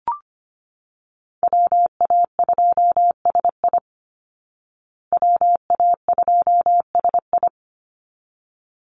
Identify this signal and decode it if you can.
Morse code, heard as a keyed tone, about 25 wpm, 700 Hz. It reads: WA2SI WA2SI
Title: Call Signs Repeated 3x - 25wpm